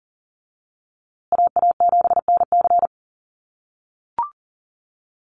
Morse code, heard as a keyed tone, about 40 wpm, 700 Hz. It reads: AA7NC
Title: USA Call Signs Repeated 3x 40wpm